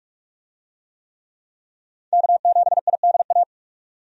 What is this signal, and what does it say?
Morse code, heard as a keyed tone, about 46 wpm, 700 Hz. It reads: K7IDA
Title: USA Call Signs - 5 Characters - 45wpm